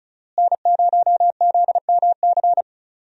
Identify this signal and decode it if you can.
Morse code, heard as a keyed tone, about 34 wpm, 700 Hz. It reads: N0ZMC
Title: USA Call Signs - 5 Characters - Repeated 3x - 35wpm